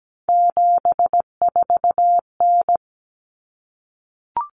Morse code, heard as a keyed tone, about 17 wpm, 690 Hz. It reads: 74N
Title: Common Cut Numbers - 3-Digit 17wpm